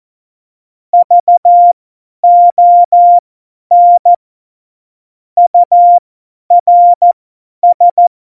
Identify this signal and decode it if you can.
Morse code, heard as a keyed tone, about 14 wpm, spaced as for 10 wpm, 700 Hz. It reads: VON URS
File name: Morsekurs / Witze / Witz 21. Mai 2019, 50 F70